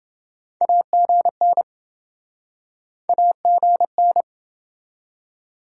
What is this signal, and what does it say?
Morse code, heard as a keyed tone, about 26 wpm, 700 Hz. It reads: AGN AGN